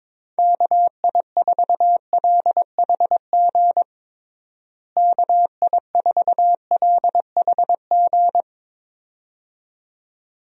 Morse code, heard as a keyed tone, about 22 wpm, 700 Hz. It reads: KI4LHG KI4LHG
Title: USA Call Signs - 6 Characters - Repeated 2x - 22wpm